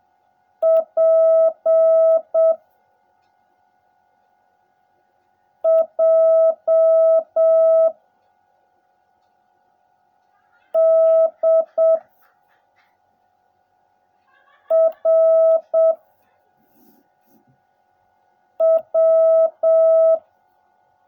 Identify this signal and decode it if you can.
Morse code, heard as a keyed tone, about 7 wpm, 650 Hz. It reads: P J D R W
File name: jh8cbh / 2021cw / cw16a